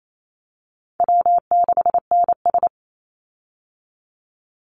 Morse code, heard as a keyed tone, about 28 wpm, 700 Hz. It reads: W6NS
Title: Easy Call Signs Repeated 2x - 28wpm